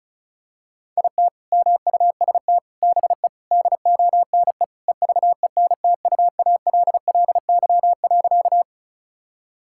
Morse code, heard as a keyed tone, about 36 wpm, 700 Hz. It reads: IT MUST BE DONE EVENTUALLY.
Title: Sentences from Top 1000 Words - Review Part 6 - 35wpm